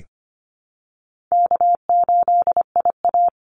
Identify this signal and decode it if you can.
Morse code, heard as a keyed tone, about 25 wpm, 700 Hz. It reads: K8IA
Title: Call Signs - 25wpm